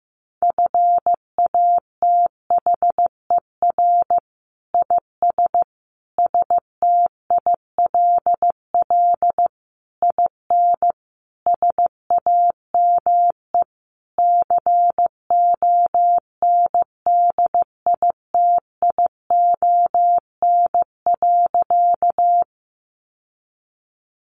Morse code, heard as a keyed tone, about 15 wpm, 700 Hz. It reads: FATHER IS STILL IN SAME CONDITION.
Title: Sentences from Top 1000 Words 15wpm